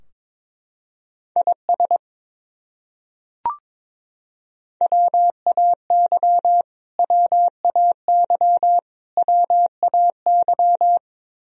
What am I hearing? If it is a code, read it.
Morse code, heard as a keyed tone, about 22 wpm, 700 Hz. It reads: IS WAY WAY WAY
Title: Top 100 Words Repeated 22wpm